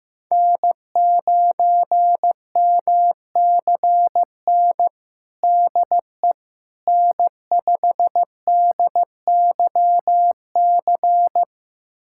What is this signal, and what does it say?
Morse code, heard as a keyed tone, about 15 wpm, 700 Hz. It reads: N9MCN DE N5DYC
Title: USA Callsign DE Callsign - 5 Characters - 15wpm